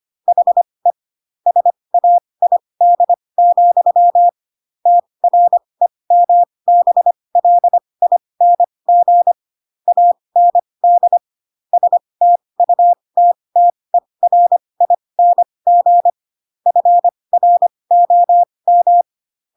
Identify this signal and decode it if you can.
Morse code, heard as a keyed tone, about 25 wpm, spaced as for 20 wpm, 700 Hz. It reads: HE SAID, TREMBLING AND STUTTERING FROM
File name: Pinocchio / Pinocchio ch1 15 25 ik1hge